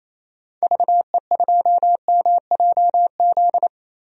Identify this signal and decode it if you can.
Morse code, heard as a keyed tone, about 28 wpm, 700 Hz. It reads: VE2MJZ
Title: DXCC-100 Call Signs - 6 Characters - Repeated 2x - 28wpm